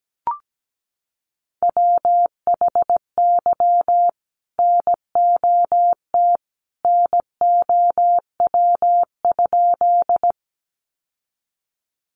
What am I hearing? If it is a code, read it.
Morse code, heard as a keyed tone, about 17 wpm, 700 Hz. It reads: WHY NOT NOW?
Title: Sentences from Top 600 Words 17wpm